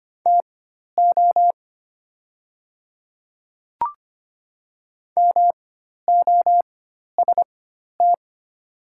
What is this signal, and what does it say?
Morse code, heard as a keyed tone, about 25 wpm, spaced as for 12 wpm, 700 Hz. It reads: TO MOST
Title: 4x Character Spacing - Top 100 Words - 25wpm